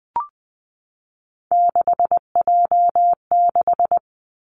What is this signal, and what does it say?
Morse code, heard as a keyed tone, about 20 wpm, 700 Hz. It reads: 6J6
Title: Three Challenging Letter-Number Combinations 20wpm